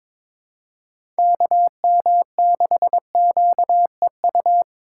Morse code, heard as a keyed tone, about 22 wpm, 700 Hz. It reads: KM6QEU
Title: USA Call Signs - 6 Characters - 2x Character Spacing - 22wpm